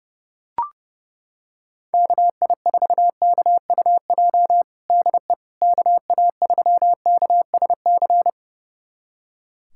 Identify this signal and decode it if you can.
Morse code, heard as a keyed tone, about 30 wpm, 700 Hz. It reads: KI4KUJ DE KA3KSC
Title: USA Callsign DE Callsign 30wpm